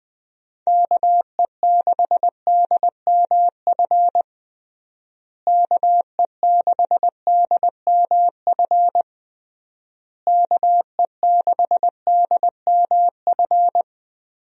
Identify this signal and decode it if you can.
Morse code, heard as a keyed tone, about 20 wpm, 700 Hz. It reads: KE6DMF KE6DMF KE6DMF
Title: USA Call Signs - 6 Characters - Repeated 3x - 20wpm